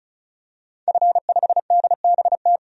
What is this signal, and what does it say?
Morse code, heard as a keyed tone, about 36 wpm, 700 Hz. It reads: F5DBT
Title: DXCC-100 Call Signs - 5 Characters - 35wpm